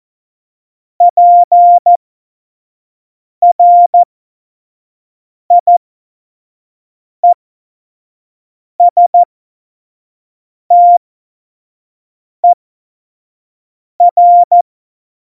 Morse code, heard as a keyed tone, about 14 wpm, spaced as for 5 wpm, 700 Hz. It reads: PRIESTER
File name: Morsekurs / Witze / Witz 15. Dezember, F70